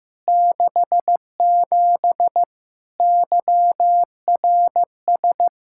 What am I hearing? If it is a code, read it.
Morse code, heard as a keyed tone, about 15 wpm, 700 Hz. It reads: 67 YRS
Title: Practice QSO - Exchange 3 - More Info - 15wpm